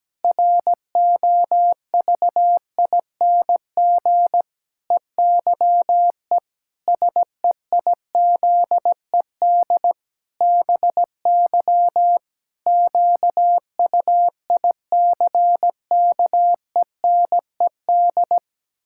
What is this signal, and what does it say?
Morse code, heard as a keyed tone, about 17 wpm, 700 Hz. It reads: ROVING EYE SEIZED BY QUICKENED
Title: Pangram Challenge v2 - 17wpm